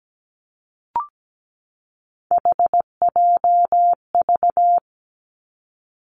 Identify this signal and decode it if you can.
Morse code, heard as a keyed tone, about 17 wpm, 700 Hz. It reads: HJV
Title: Three Challenging Letter-Number Combinations 17wpm